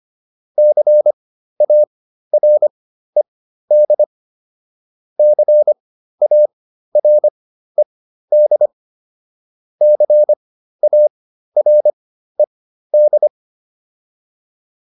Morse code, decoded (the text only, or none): CARED CARED CARED